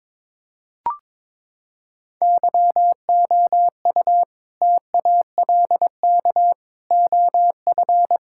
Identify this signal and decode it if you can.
Morse code, heard as a keyed tone, about 22 wpm, 700 Hz. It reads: YOU TALK OF
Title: Sentences from Top 1600 Words 22wpm